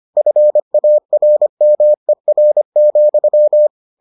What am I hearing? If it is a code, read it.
Morse code, heard as a keyed tone, about 25 wpm, 600 Hz. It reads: FARMER,